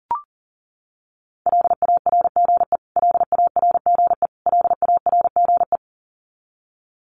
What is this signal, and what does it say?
Morse code, heard as a keyed tone, about 40 wpm, 700 Hz. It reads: LARGE LARGE LARGE
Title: Top 200 Words Repeated 40wpm